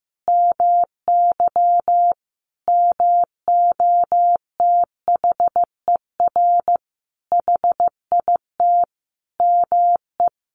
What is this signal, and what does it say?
Morse code, heard as a keyed tone, about 14 wpm, 700 Hz. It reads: MY MOTHER HIT ME
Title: Sentences from Top 1400 Words 15wpm